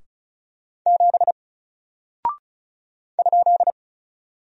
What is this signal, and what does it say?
Morse code, heard as a keyed tone, about 36 wpm, 700 Hz. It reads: 7 ?
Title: Single Character (Letter:Number:Punctuation:Prosign) 35wpm